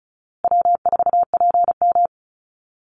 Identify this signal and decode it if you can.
Morse code, heard as a keyed tone, about 34 wpm, 700 Hz. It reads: W4PM
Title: Easy Call Signs Repeated 3x - 35wpm